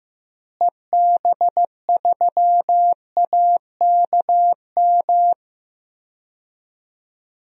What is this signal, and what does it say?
Morse code, heard as a keyed tone, about 15 wpm, 700 Hz. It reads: EB3AKM
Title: DXCC-100 Call Signs - 6 Characters - Repeated 3x - 15wpm